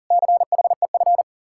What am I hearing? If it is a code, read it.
Morse code, heard as a keyed tone, about 49 wpm, 700 Hz. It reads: CHEF
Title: Speed-Racing Common QSO Phrases - 40wpm